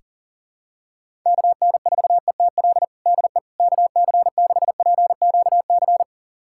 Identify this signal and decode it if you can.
Morse code, heard as a keyed tone, about 40 wpm, 700 Hz. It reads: KN4ETL DE KC6PQC
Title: USA Callsign DE Callsign 40wpm